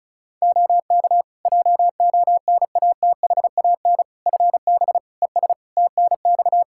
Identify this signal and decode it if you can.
Morse code, heard as a keyed tone, about 36 wpm, 700 Hz. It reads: OK JONATHAN FB ES TNX
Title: Practice QSO - Exchange 2 - Info - 35wpm